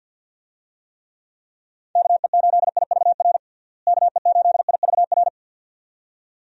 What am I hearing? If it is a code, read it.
Morse code, heard as a keyed tone, about 48 wpm, 700 Hz. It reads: KE8IVR KE8IVR
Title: USA Call Signs - 6 Characters - Repeated 2x - 50wpm